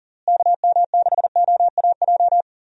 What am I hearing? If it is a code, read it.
Morse code, heard as a keyed tone, about 40 wpm, 700 Hz. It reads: KM6OAJ
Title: USA Call Signs - 6 Characters - Repeated 3x - 40wpm